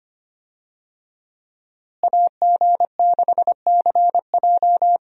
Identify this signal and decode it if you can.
Morse code, heard as a keyed tone, about 25 wpm, 700 Hz. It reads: AG6CJ
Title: USA Call Signs - 5 Characters - Repeated 3x - 25wpm